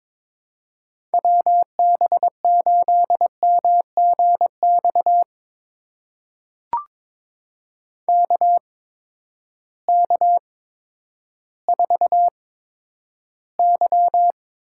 Morse code, heard as a keyed tone, about 21 wpm, spaced as for 11 wpm, 700 Hz. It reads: WB8MGX KK4Y
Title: USA Call Signs - 6 Characters - 8x Character Spacing - 22wpm